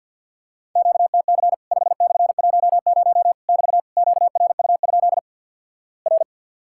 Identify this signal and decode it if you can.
Morse code, heard as a keyed tone, about 50 wpm, 700 Hz. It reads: QTC 5/10 <BT> QRU? R
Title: Worked All Europe DX Contest - QTC Exchange - 50wpm